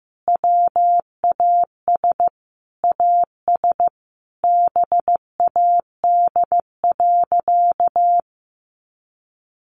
Morse code, heard as a keyed tone, about 15 wpm, 700 Hz. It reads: WAS AS BAD.
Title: Sentences from Top 1200 Words 15wpm